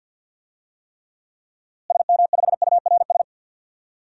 Morse code, heard as a keyed tone, about 46 wpm, 700 Hz. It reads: SM5URS
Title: DXCC-100 Call Signs - 6 Characters - Repeated 3x - 50wpm